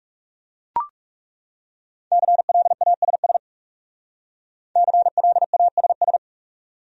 Morse code, heard as a keyed tone, about 39 wpm, 700 Hz. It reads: CLASS CLASS